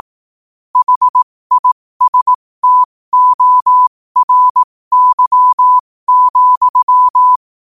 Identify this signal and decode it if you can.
Morse code, heard as a keyed tone, about 18 wpm, spaced as for 15 wpm, 1000 Hz. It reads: HISTORY,
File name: qotdfiles / 15wpm